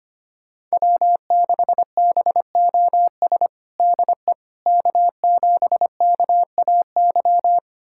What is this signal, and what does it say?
Morse code, heard as a keyed tone, about 25 wpm, 700 Hz. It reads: W6BOS DE K7KAY
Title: USA Callsign DE Callsign - 5 Characters - 25wpm